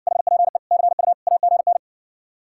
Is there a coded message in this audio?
Yes: Morse code from 0.1 to 1.8 s, heard as a keyed tone, about 70 wpm, 700 Hz. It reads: HPE CU AGN